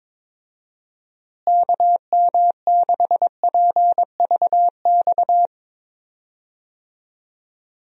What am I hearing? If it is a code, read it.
Morse code, heard as a keyed tone, about 22 wpm, 700 Hz. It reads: KM6PVX
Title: USA Call Signs - 6 Characters - Repeated 3x - 22wpm